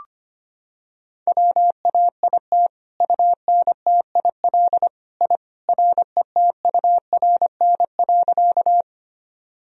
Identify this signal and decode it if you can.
Morse code, heard as a keyed tone, about 25 wpm, 700 Hz. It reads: WAIT UNTIL I RETURN.
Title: Sentences from Top 1000 Words - Review Part 1 - 25wpm